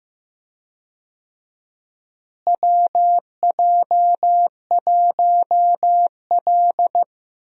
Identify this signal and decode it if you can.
Morse code, heard as a keyed tone, about 15 wpm, 700 Hz. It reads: WJ1L
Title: USA Call Signs - 4 Characters - Repeated 2x - 15wpm